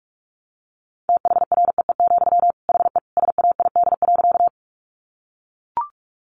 Morse code, heard as a keyed tone, about 44 wpm, 700 Hz. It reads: THREE, HE SAID.
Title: Sentences from Top 200 Words 45wpm